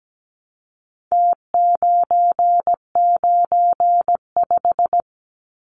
Morse code, heard as a keyed tone, about 17 wpm, 700 Hz. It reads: T995
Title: Numbers - Contest Style - 4-Digit 17wpm